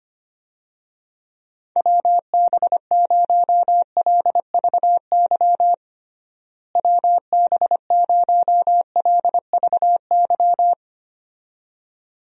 Morse code, heard as a keyed tone, about 25 wpm, 700 Hz. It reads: WB0LVY WB0LVY
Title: USA Call Signs - 6 Characters - Repeated 2x - 25wpm